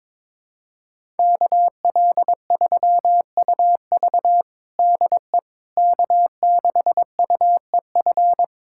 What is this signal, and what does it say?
Morse code, heard as a keyed tone, about 21 wpm, 700 Hz. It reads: KL3UV DE K6UEF
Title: USA Callsign DE Callsign - 5 Characters - 22wpm